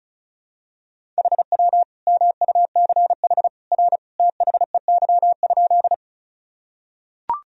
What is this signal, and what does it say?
Morse code, heard as a keyed tone, about 34 wpm, 700 Hz. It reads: HW MUCH R THEY?
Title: Sentences from Top 200 Words and Abbreviations 35wpm